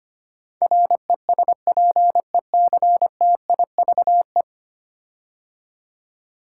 Morse code, heard as a keyed tone, about 25 wpm, 700 Hz. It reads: RESPECTIVE
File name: 10 Letter Words - First 3 Letters Given - 25wpm